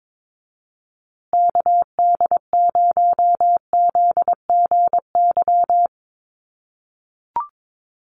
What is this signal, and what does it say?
Morse code, heard as a keyed tone, about 22 wpm, 700 Hz. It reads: KD0ZGY
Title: USA Call Signs Repeated 3x 22wpm